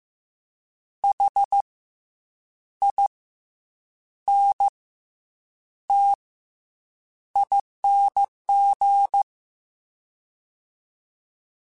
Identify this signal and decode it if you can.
Morse code, heard as a keyed tone, about 15 wpm, spaced as for 7 wpm, 800 Hz. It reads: HINTING
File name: Special INGs-mp3s / Special INGs-mp3s / Fast ing @13